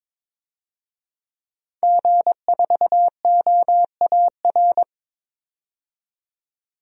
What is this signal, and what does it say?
Morse code, heard as a keyed tone, about 22 wpm, 700 Hz. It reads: G4OAR
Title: DXCC-100 Call Signs - 5 Characters - 22wpm